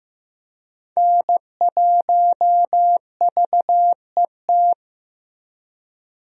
Morse code, heard as a keyed tone, about 15 wpm, 700 Hz. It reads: N1VET
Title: USA Call Signs - 5 Characters - Repeated 3x - 15wpm